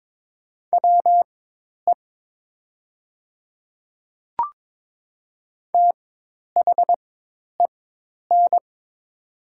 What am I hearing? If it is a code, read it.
Morse code, heard as a keyed tone, about 22 wpm, spaced as for 10 wpm, 700 Hz. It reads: WE THEN
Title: 4x Character Spacing - Top 300 Words - 22wpm